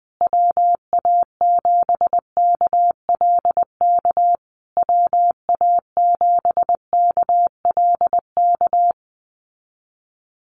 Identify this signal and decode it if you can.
Morse code, heard as a keyed tone, about 20 wpm, 700 Hz. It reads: WA7KLK WA7KLK
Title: USA Call Signs Repeated 3x 20wpm